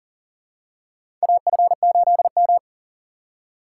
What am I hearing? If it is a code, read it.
Morse code, heard as a keyed tone, about 40 wpm, 700 Hz. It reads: AF8M
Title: USA Call Signs - 4 Characters - 2x Character Spacing - 40wpm